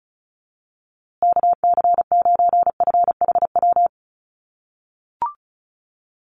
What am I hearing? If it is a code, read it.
Morse code, heard as a keyed tone, about 34 wpm, 700 Hz. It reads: KC9FHW
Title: USA Call Signs Repeated 3x 35wpm